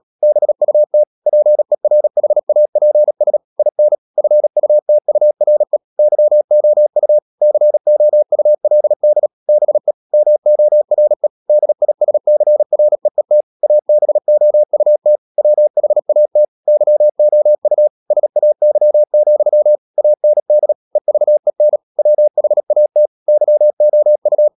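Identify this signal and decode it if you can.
Morse code, heard as a keyed tone, about 37 wpm, 600 Hz. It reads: BUT PERHAPS IN FUTURE YOU COULD BE MORE DISCREET ABOUT WHAT YOU SAY, AND EVEN WHAT YOU
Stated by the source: Morse code practice at 35–39 wpm sending English text of Ken Follett's novel Never, from CW med Gnister.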